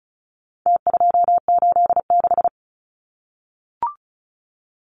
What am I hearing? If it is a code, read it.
Morse code, heard as a keyed tone, about 34 wpm, 700 Hz. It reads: T286
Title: Numbers - Contest Style - 4-Digit 35wpm